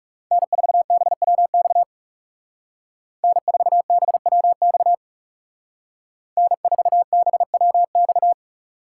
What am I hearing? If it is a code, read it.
Morse code, heard as a keyed tone, about 39 wpm, 700 Hz. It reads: N4BWX N4BWX N4BWX